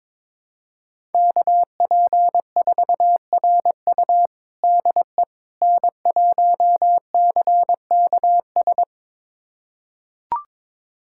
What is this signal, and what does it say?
Morse code, heard as a keyed tone, about 22 wpm, 700 Hz. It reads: KP4RU DE N1CKS
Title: USA Callsign DE Callsign - 5 Characters - 22wpm